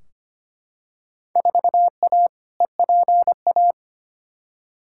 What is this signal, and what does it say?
Morse code, heard as a keyed tone, about 25 wpm, 700 Hz. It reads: 4A EPA
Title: Field Day Exchanges - 25wpm